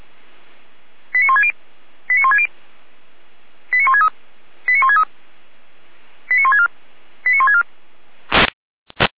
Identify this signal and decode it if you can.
Five-tone selective call (ZVEI), heard as quick runs of five tones, tones of about 70 ms. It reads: 89160 89160 89163 89163 89164 89164